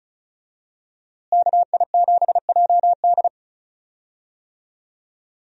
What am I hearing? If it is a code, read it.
Morse code, heard as a keyed tone, about 34 wpm, 700 Hz. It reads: KI7JD
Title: USA Call Signs - 5 Characters - Repeated 2x - 35wpm